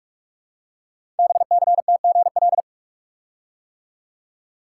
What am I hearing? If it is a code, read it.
Morse code, heard as a keyed tone, about 44 wpm, 700 Hz. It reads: BCTGL
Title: Code Groups - 5 Letters - 45wpm